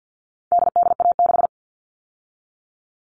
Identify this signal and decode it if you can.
Morse code, heard as a keyed tone, about 50 wpm, 700 Hz. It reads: DDA6
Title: All Cut Numbers - 4-Digit 50wpm